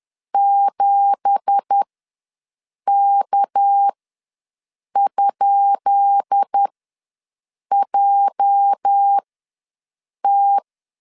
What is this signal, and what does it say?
Morse code, heard as a keyed tone, about 10 wpm, spaced as for 6 wpm, 800 Hz. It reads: 7K?JT
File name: LEZIONI / 10 SIGNS / 10B INT ws 10wpm 4134